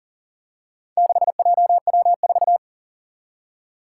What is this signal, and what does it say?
Morse code, heard as a keyed tone, about 40 wpm, 700 Hz. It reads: 6JW4
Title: Four Challenging Letter-Number Combinations 40wpm